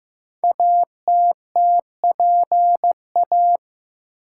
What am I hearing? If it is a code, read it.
Morse code, heard as a keyed tone, about 15 wpm, 700 Hz. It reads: ATTPA
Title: Code Groups - 5 Letters - 15wpm